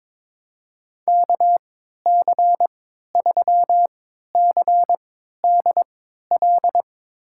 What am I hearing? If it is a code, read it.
Morse code, heard as a keyed tone, about 21 wpm, spaced as for 12 wpm, 700 Hz. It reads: KC3CDL
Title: USA Call Signs - 6 Characters - 3x Character Spacing - 22wpm